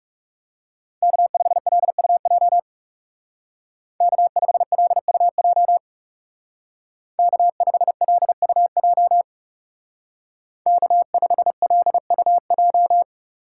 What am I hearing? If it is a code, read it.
Morse code, heard as a keyed tone, about 35 wpm, 700 Hz. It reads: K5LUJ K5LUJ K5LUJ K5LUJ
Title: Speed Racing - USA Call Signs 30wpm